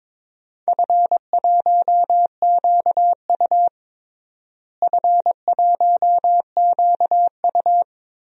Morse code, heard as a keyed tone, about 22 wpm, 700 Hz. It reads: F1QU F1QU
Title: DXCC-100 Call Signs - 4 Characters - Repeated 2x - 22wpm